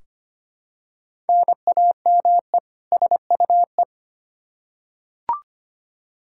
Morse code, heard as a keyed tone, about 25 wpm, 700 Hz. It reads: NAME SUE
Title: Practice QSO - Exchange 1 - Beginning - 25wpm